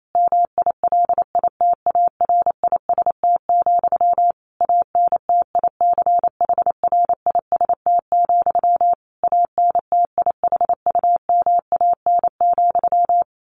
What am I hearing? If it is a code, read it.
Morse code, heard as a keyed tone, about 28 wpm, 700 Hz. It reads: MILITARIST, ANTICHRIST, ANTIHUMAN,
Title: A Sound Of Thunder 28wpm